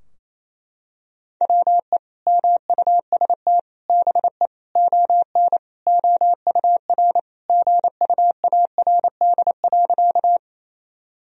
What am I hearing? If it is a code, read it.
Morse code, heard as a keyed tone, about 28 wpm, 700 Hz. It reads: WE MUST BE ON OUR GUARD.
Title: Sentences from Top 1000 Words - Review Part 4 - 28wpm